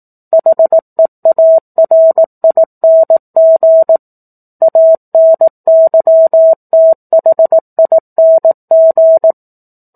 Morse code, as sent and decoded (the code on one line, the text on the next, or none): .... .- .-. .. -. --. / .- -. -.-- - .... .. -. --.
HEARING ANYTHING